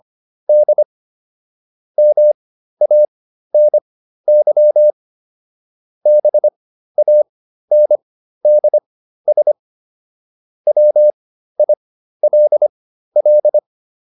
Morse code, heard as a keyed tone, about 25 wpm, spaced as for 13 wpm, 600 Hz. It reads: D MANY BANDS WILL